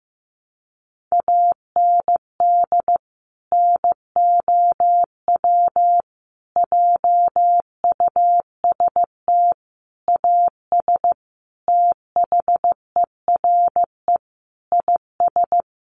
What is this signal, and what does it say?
Morse code, heard as a keyed tone, about 15 wpm, 700 Hz. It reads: AND NOW JUST AS THERE IS
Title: Sentences from Top 1200 Words 15wpm